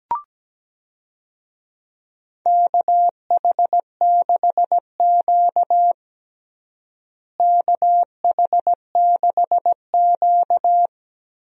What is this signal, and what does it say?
Morse code, heard as a keyed tone, about 17 wpm, 700 Hz. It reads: KH6Q KH6Q
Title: DXCC-100 Call Signs - 4 Characters - Repeated 3x - 17wpm